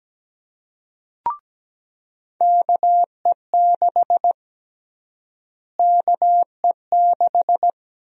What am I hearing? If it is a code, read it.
Morse code, heard as a keyed tone, about 17 wpm, 700 Hz. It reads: KE6 KE6
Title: USA Call Sign Prefixes - Repeated 2x - 17wpm